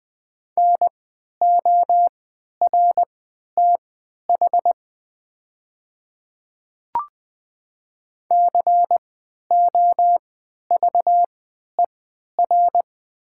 Morse code, heard as a keyed tone, about 20 wpm, spaced as for 11 wpm, 700 Hz. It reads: NORTH COVER
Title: 3x Character Spacing - Top 500 Words - 20wpm